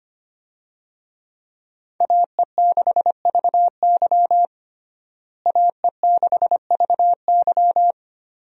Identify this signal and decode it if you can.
Morse code, heard as a keyed tone, about 25 wpm, 700 Hz. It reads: AE6VY AE6VY
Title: USA Call Signs - 5 Characters - Repeated 3x - 25wpm